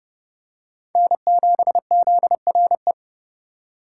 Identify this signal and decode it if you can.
Morse code, heard as a keyed tone, about 30 wpm, 700 Hz. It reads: N7ZRE
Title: USA Call Signs - 5 Characters - 4x Character Spacing - 30wpm